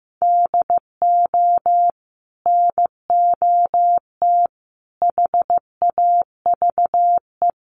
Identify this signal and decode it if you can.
Morse code, heard as a keyed tone, about 15 wpm, 700 Hz. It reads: DO NOT HAVE
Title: Sets of 4 Words 15wpm